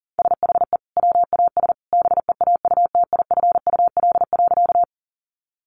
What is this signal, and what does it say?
Morse code, heard as a keyed tone, about 40 wpm, 700 Hz. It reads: SHE WAS BEAUTIFUL.
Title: Sentences from Top 900 Words 40wpm